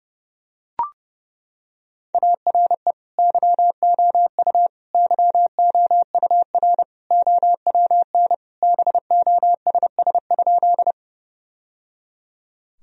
Morse code, heard as a keyed tone, about 30 wpm, 700 Hz. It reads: ARE YOU YOUR OWN BOSS?
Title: Sentences from Top 1000 Words - Review Part 4 - 30wpm